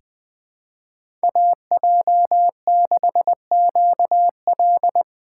Morse code, heard as a keyed tone, about 20 wpm, 700 Hz. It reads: AJ6QL
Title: K1USN SST QSO Practice - 20wpm